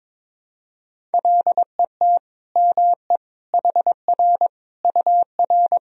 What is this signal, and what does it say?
Morse code, heard as a keyed tone, about 22 wpm, 700 Hz. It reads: LET ME HR UR
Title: Sentences from Top 300 Words and Abbreviations 22wpm